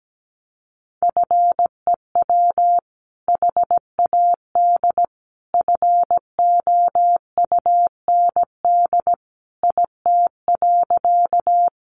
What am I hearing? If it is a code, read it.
Morse code, heard as a keyed tone, about 17 wpm, 700 Hz. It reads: FEW HAD FOUND IT.
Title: Sentences from Top 300 Words 17wpm